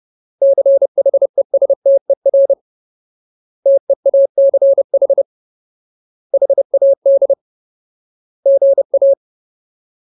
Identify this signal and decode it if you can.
Morse code, heard as a keyed tone, about 30 wpm, 550 Hz. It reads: CHESTER TEACH HAD GA
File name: Ultimate Word List / 30WPM Ultimate